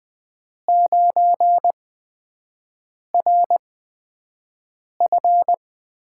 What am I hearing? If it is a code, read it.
Morse code, heard as a keyed tone, about 20 wpm, spaced as for 5 wpm, 700 Hz. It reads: 9RF
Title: USA Call Signs - 5 Characters - 8x Character Spacing - 20wpm